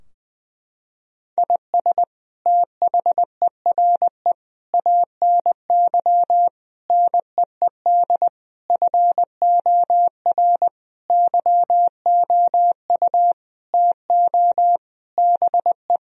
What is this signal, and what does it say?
Morse code, heard as a keyed tone, about 20 wpm, 700 Hz. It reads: IS THERE ANY NEED FOR YOU TO BE
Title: Sentences from Top 1000 Words - Review Part 4 - 20wpm